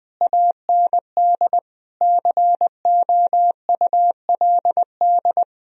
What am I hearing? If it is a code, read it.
Morse code, heard as a keyed tone, about 20 wpm, 700 Hz. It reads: AND COULD
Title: Sentences from Top 1000 Words - Review Part 3 - 20wpm